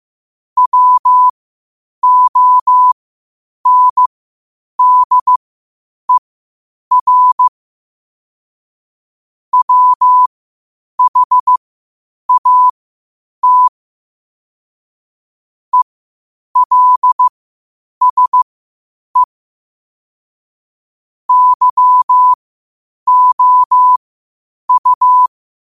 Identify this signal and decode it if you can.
Morse code, heard as a keyed tone, about 15 wpm, spaced as for 8 wpm, 1000 Hz. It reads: WONDER WHAT ELSE YOU
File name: qotdfiles / 8wpm